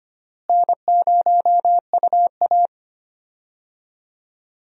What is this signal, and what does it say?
Morse code, heard as a keyed tone, about 25 wpm, 700 Hz. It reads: N0UA
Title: USA Call Signs - 4 Characters - Repeated 3x - 25wpm